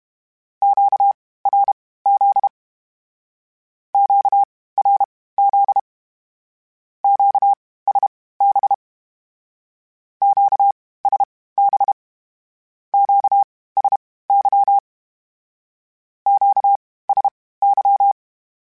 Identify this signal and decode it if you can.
Morse code, heard as a keyed tone, about 32 wpm, spaced as for 18 wpm, 800 Hz. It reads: QRZ QRZ QSB QSB QSY QSY
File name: (spaced) / Q-Calls Spaced-mp3s / Q Calls Twice sp @30